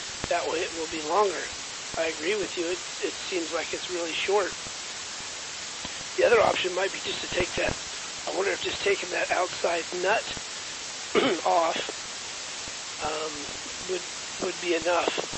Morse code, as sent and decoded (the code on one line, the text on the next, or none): none